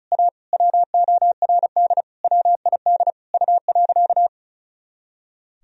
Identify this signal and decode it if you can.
Morse code, heard as a keyed tone, about 36 wpm, 700 Hz. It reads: A WORD WID U.
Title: Sentences from Top 300 Words and Abbreviations 35wpm